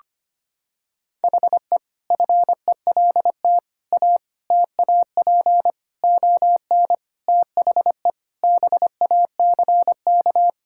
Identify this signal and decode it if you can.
Morse code, heard as a keyed tone, about 25 wpm, 700 Hz. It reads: HE FELT A TAP ON THE BACK
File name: practice / 25wpm / Sentences with 4-Word Phrases - 25wpm